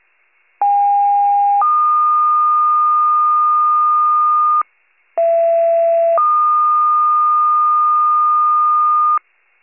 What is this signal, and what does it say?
Two-tone page, heard as a one-second tone, then a three-second tone, about 810 then 1220 Hz, about 670 then 1220 Hz. It is heard from 0.6 to 9.2 s.
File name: wcalert